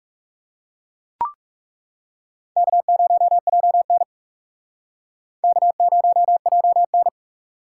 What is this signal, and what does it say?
Morse code, heard as a keyed tone, about 35 wpm, 700 Hz. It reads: K0JN K0JN
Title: Speed Racing - USA 1x2 Call Signs 30wpm